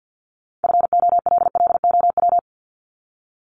Morse code, heard as a keyed tone, about 50 wpm, 700 Hz. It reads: FOLLOW
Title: Top 200 Words Repeated 50wpm